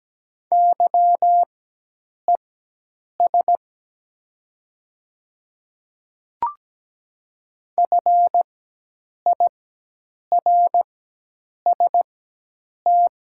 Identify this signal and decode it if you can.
Morse code, heard as a keyed tone, about 17 wpm, spaced as for 8 wpm, 700 Hz. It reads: YES FIRST
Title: 4x Character Spacing - Top 500 Words - 17wpm